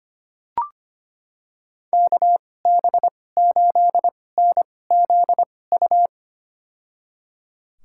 Morse code, heard as a keyed tone, about 25 wpm, spaced as for 18 wpm, 700 Hz. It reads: KB8NZU
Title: USA Call Signs - 6 Characters - 2x Character Spacing - 25wpm